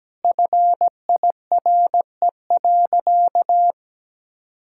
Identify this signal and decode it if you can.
Morse code, heard as a keyed tone, about 17 wpm, 700 Hz. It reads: FIRE.
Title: Sentences from Top 1600 Words 17wpm